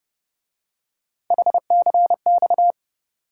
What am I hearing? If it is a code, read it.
Morse code, heard as a keyed tone, about 29 wpm, 700 Hz. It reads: HCX